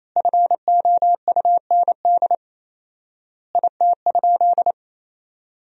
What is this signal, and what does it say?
Morse code, heard as a keyed tone, about 28 wpm, 700 Hz. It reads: FOUND IT?
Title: 4x Word-Spacing - Sentences from Top 300 Words - 28wpm